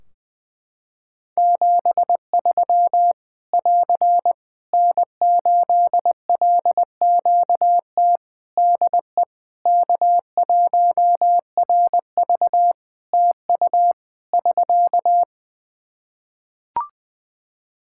Morse code, heard as a keyed tone, about 20 wpm, 700 Hz. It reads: 73 <AR> N8LQT DE K1RV TU <SK>
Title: K1USN SST QSO Practice 20wpm